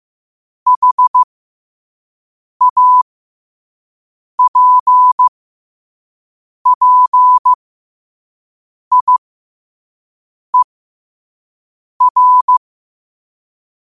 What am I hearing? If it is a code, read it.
Morse code, heard as a keyed tone, about 15 wpm, spaced as for 5 wpm, 1000 Hz. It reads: HAPPIER
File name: qotdfiles / 5wpm